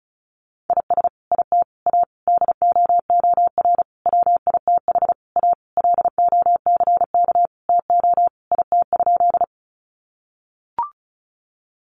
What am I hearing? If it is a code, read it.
Morse code, heard as a keyed tone, about 34 wpm, 700 Hz. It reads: IS IT A DOOR WITH A LOCK TO IT?
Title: Sentences from Top 1300 Words 35wpm